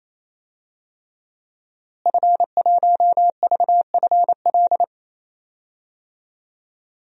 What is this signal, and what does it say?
Morse code, heard as a keyed tone, about 28 wpm, 700 Hz. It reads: F1VFL
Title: DXCC-100 Call Signs - 5 Characters - Repeated 3x - 28wpm